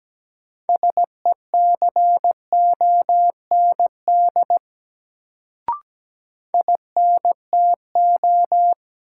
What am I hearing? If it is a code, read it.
Morse code, heard as a keyed tone, about 17 wpm, 700 Hz. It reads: SECOND INTO
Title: Sets of 2 Words - New Word Drill - 17wpm